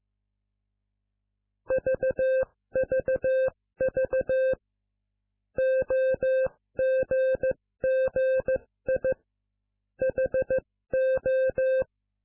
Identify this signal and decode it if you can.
Morse code, heard as a keyed tone, about 15 wpm, spaced as for 13 wpm, 520 Hz. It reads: VVV OGGI HO